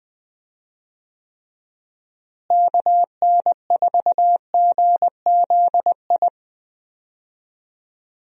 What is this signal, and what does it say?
Morse code, heard as a keyed tone, about 20 wpm, 700 Hz. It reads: KN4GZI
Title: USA Call Signs - 6 Characters - Repeated 3x - 20wpm